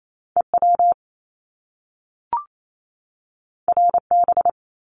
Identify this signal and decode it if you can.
Morse code, heard as a keyed tone, about 28 wpm, 700 Hz. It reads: EW RB
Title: Top Two Letter Combinations in Words 28wpm